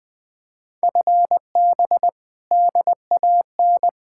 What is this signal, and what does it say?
Morse code, heard as a keyed tone, about 24 wpm, 700 Hz. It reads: FB DAN
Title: Speed-Racing Common QSO Phrases - 20wpm